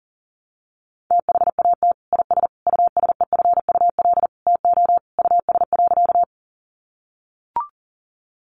Morse code, heard as a keyed tone, about 40 wpm, 700 Hz. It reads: THAT IS USEFUL TO US.
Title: Sentences from Top 700 Words 40wpm